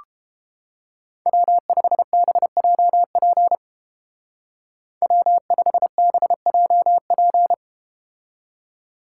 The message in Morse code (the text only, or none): W5BJP W5BJP